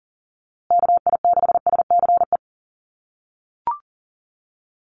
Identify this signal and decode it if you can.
Morse code, heard as a keyed tone, about 40 wpm, 700 Hz. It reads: KI6SCE
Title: USA Call Signs Repeated 2x 40wpm